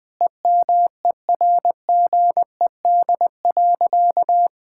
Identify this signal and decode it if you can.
Morse code, heard as a keyed tone, about 20 wpm, 700 Hz. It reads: EMERGED.
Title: Sentences with 5-Word Phrases - Spoken Beginning - 20wpm